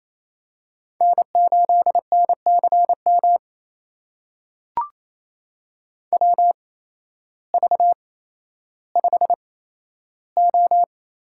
Morse code, heard as a keyed tone, about 28 wpm, spaced as for 13 wpm, 700 Hz. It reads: N8NCM WV5O
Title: USA Call Signs - 8x Character Spacing - 28wpm